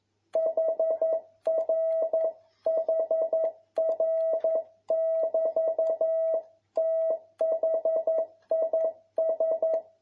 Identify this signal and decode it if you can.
Morse code, heard as a keyed tone, about 11 wpm, 650 Hz. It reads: HRHR<BT>THIS
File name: jh8cbh / 2021cw